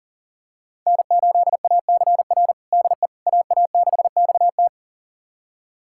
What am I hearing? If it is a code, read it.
Morse code, heard as a keyed tone, about 40 wpm, 700 Hz. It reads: N8ACR DE AA6XT
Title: USA Callsign DE Callsign 40wpm